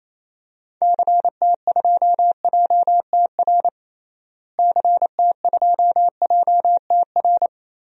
Morse code, heard as a keyed tone, about 27 wpm, 700 Hz. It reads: CT2JTR CT2JTR
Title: DXCC-100 Call Signs - 6 Characters - Repeated 3x - 28wpm